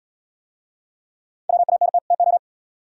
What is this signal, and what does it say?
Morse code, heard as a keyed tone, about 75 wpm, 700 Hz. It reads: WANT TO